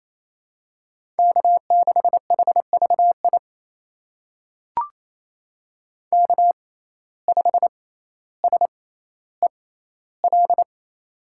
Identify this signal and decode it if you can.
Morse code, heard as a keyed tone, about 28 wpm, spaced as for 14 wpm, 700 Hz. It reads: K6HVI K5SEL
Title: USA Call Signs - 5 Characters - 6x Character Spacing - 28wpm